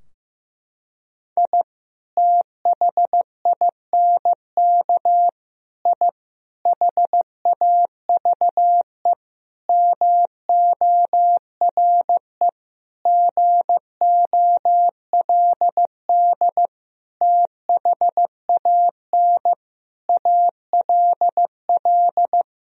Morse code, heard as a keyed tone, about 15 wpm, 700 Hz. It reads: I THINK I HAVE MORE GOLD THAN ALL
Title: Sentences from Top 1000 Words - Review Part 3 - 15wpm